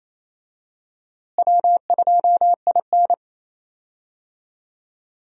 Morse code, heard as a keyed tone, about 28 wpm, 700 Hz. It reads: W2IN
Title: USA Call Signs - 4 Characters - Repeated 2x - 28wpm